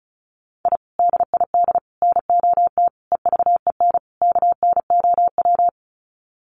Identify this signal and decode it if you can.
Morse code, heard as a keyed tone, about 34 wpm, 700 Hz. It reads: I DID NOT EVEN KNOW
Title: Sets of 5 Words Encore 35wpm